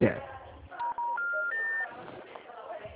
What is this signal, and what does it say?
Special information tone, with speech.